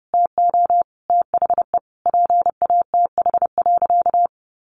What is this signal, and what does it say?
Morse code, heard as a keyed tone, about 30 wpm, 700 Hz. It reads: TO THE PATH.
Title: A Sound Of Thunder 30wpm